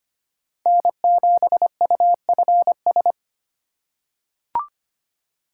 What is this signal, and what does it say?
Morse code, heard as a keyed tone, about 25 wpm, 700 Hz. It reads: N7UFS
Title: USA Call Signs - 5 Characters - 2x Character Spacing - 25wpm